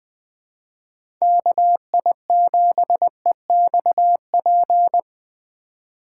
Morse code, heard as a keyed tone, about 21 wpm, 700 Hz. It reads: KI7EXP